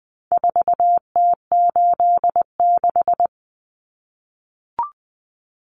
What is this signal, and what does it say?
Morse code, heard as a keyed tone, about 20 wpm, 700 Hz. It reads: 4T86
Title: Common Cut Numbers - 4-Digit 20wpm